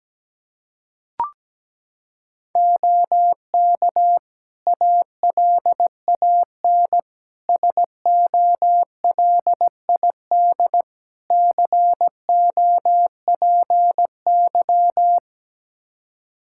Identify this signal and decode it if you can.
Morse code, heard as a keyed tone, about 17 wpm, 700 Hz. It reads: OK ALAN SOLID COPY
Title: Practice QSO - Exchange 3 - More Info - 17wpm